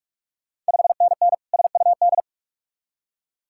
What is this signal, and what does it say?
Morse code, heard as a keyed tone, about 46 wpm, 700 Hz. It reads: 5NN SUD
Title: Ontario QSO Party Exchange 45wpm